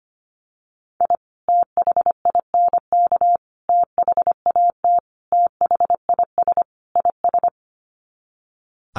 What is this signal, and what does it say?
Morse code, heard as a keyed tone, about 25 wpm, 700 Hz. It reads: I THINK THAT THIS IS
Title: Sets of 5 Words Encore2 25wpm